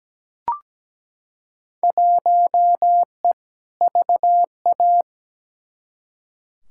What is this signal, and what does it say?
Morse code, heard as a keyed tone, about 17 wpm, 700 Hz. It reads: 1E VA
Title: Field Day Exchanges - 17wpm